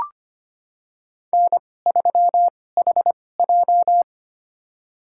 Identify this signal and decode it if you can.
Morse code, heard as a keyed tone, about 25 wpm, spaced as for 18 wpm, 700 Hz. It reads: N3HJ
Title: USA Call Signs - 4 Characters - 2x Character Spacing - 25wpm